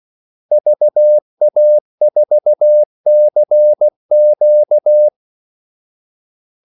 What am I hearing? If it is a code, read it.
Morse code, heard as a keyed tone, about 16 wpm, 600 Hz. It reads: VA4CQ
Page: Kaldesignaler 16 wpm